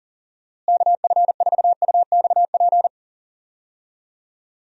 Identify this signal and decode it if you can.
Morse code, heard as a keyed tone, about 40 wpm, 700 Hz. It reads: KF4UXP
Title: USA Call Signs - 6 Characters - Repeated 3x - 40wpm